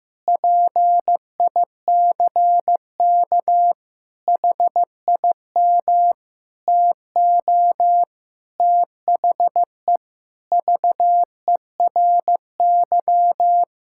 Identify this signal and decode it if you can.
Morse code, heard as a keyed tone, about 16 wpm, 700 Hz. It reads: PICK HIM TO THE VERY
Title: Sentences from Top 1600 Words 15wpm